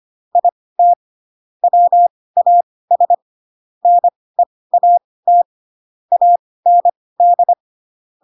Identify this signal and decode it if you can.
Morse code, heard as a keyed tone, about 25 wpm, spaced as for 18 wpm, 700 Hz. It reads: IT WAS NEAT AND